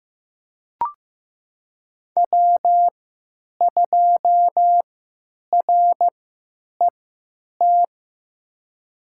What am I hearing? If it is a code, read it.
Morse code, heard as a keyed tone, about 15 wpm, spaced as for 8 wpm, 700 Hz. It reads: W2RET
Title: USA Call Signs - 5 Characters - 3x Character Spacing - 15wpm